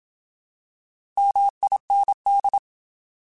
Morse code, heard as a keyed tone, about 27 wpm, 800 Hz. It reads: MIND
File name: Alphabetical / Alphabetical mp3s / Ms @25